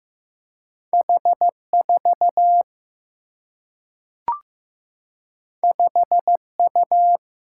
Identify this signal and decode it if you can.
Morse code, heard as a keyed tone, about 15 wpm, 700 Hz. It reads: H 4 5 U
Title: Single-Letter Single-Number 15wpm